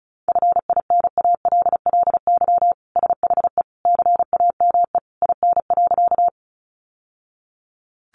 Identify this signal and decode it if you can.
Morse code, heard as a keyed tone, about 34 wpm, 700 Hz. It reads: FINALLY SHE CAME IN.
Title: Sentences from Top 1000 Words 35wpm